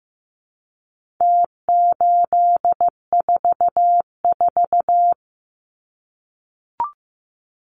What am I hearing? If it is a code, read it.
Morse code, heard as a keyed tone, about 15 wpm, 700 Hz. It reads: T844
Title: Numbers - Contest Style - 4-Digit 15wpm